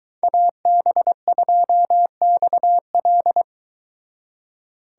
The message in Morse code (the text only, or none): AB2XL